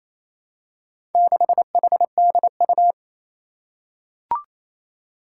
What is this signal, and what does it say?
Morse code, heard as a keyed tone, about 29 wpm, 700 Hz. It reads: 6HDU
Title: Four Challenging Letter-Number Combinations 28wpm